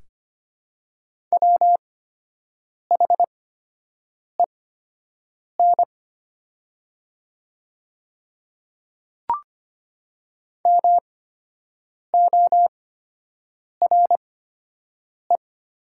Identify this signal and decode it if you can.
Morse code, heard as a keyed tone, about 25 wpm, spaced as for 7 wpm, 700 Hz. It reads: WHEN MORE
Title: 8x Character Spacing - Top 200 Words - 25wpm